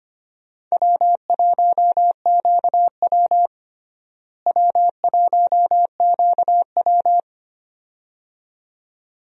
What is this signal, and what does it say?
Morse code, heard as a keyed tone, about 25 wpm, 700 Hz. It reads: W1QW W1QW
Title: USA Call Signs - 4 Characters - Repeated 2x - 25wpm